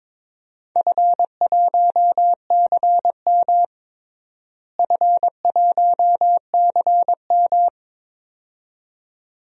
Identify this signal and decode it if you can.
Morse code, heard as a keyed tone, about 22 wpm, 700 Hz. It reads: F1CM F1CM
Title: DXCC-100 Call Signs - 4 Characters - Repeated 2x - 22wpm